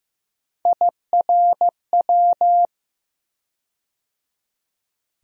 Morse code, heard as a keyed tone, about 15 wpm, 700 Hz. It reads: IRW
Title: USA Call Sign Postfixes - 3 Letters 15wpm